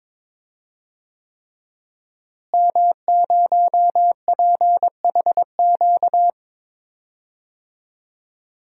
Morse code, heard as a keyed tone, about 22 wpm, 700 Hz. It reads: M0PHQ
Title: DXCC-100 Call Signs - 5 Characters - Repeated 3x - 22wpm